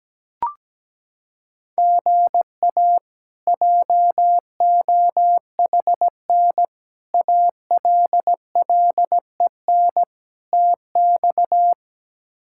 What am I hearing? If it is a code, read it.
Morse code, heard as a keyed tone, about 17 wpm, 700 Hz. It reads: GA JOHN ALLEN TX
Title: K1USN SST QSO Practice - 17wpm